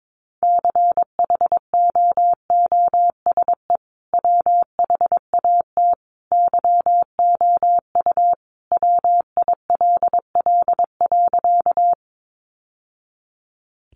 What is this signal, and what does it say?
Morse code, heard as a keyed tone, about 22 wpm, 700 Hz. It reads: CHOOSE WHAT YOU WILL.
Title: Sentences from Top 700 Words 22wpm